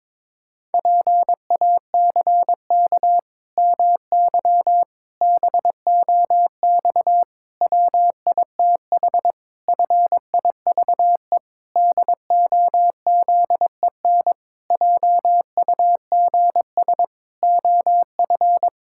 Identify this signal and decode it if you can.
Morse code, heard as a keyed tone, about 23 wpm, 700 Hz. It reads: PACK MY BOX WITH FIVE DOZEN JUGS OF
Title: Pangram Challenge v2 - 22wpm